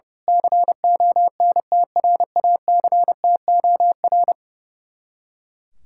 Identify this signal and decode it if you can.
Morse code, heard as a keyed tone, about 29 wpm, 700 Hz. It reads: CONTRACTOR
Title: Speed-Racing Common QSO Phrases - 30wpm